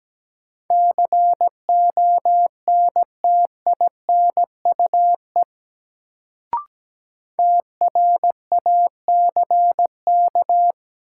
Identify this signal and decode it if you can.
Morse code, heard as a keyed tone, about 17 wpm, 700 Hz. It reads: CONTINUE TRACK
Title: Sentences from Top 1100 Words - New Word Drill Alt - 17wpm